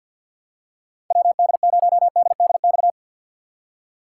Morse code, heard as a keyed tone, about 52 wpm, 700 Hz. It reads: WD0DDX
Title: USA Call Signs - 6 Characters - 6x Character Spacing - 50wpm